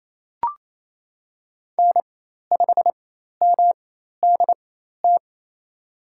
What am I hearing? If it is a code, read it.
Morse code, heard as a keyed tone, about 28 wpm, spaced as for 13 wpm, 700 Hz. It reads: N5MDT
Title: USA Call Signs - 5 Characters - 4x Character Spacing - 28wpm